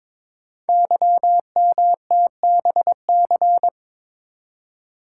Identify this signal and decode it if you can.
Morse code, heard as a keyed tone, about 22 wpm, 700 Hz. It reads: YMTBC
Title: Code Groups - 5 Letters - 22wpm